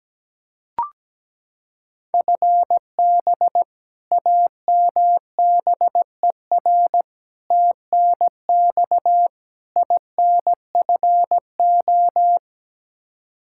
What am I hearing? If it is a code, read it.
Morse code, heard as a keyed tone, about 17 wpm, 700 Hz. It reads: FB AMBER TNX INFO
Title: Practice QSO - Exchange 3 - More Info - 17wpm